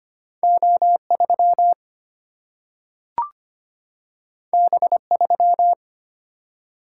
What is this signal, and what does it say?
Morse code, heard as a keyed tone, about 25 wpm, 700 Hz. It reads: O3 B3
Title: Single-Letter Single-Number 25wpm